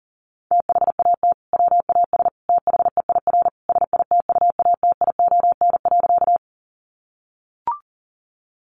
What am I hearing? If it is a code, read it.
Morse code, heard as a keyed tone, about 37 wpm, 700 Hz. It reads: THAT WAS THEIR SITUATION.
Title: Sentences from Top 1200 Words 40wpm